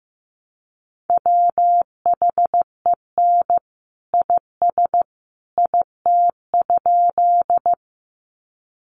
Morse code, heard as a keyed tone, about 14 wpm, 700 Hz. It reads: WHEN IS IT?
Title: Sentences from Top 100 Words 15wpm